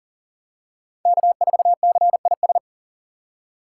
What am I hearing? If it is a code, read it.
Morse code, heard as a keyed tone, about 40 wpm, 700 Hz. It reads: K4CIS
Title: USA Call Signs - 2x Character Spacing - 40wpm